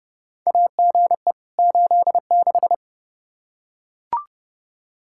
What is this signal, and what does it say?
Morse code, heard as a keyed tone, about 30 wpm, 700 Hz. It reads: AGE 86
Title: K1USN SST QSO Practice 30wpm